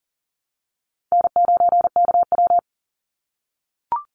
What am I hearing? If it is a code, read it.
Morse code, heard as a keyed tone, about 40 wpm, 700 Hz. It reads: N9KW
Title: NA SOTA Chaser Callsigns 40wpm